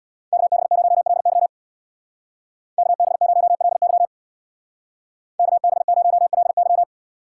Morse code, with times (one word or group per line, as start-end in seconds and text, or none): 0.3-1.5 KB0LY
2.8-4.1 KB0LY
5.4-6.8 KB0LY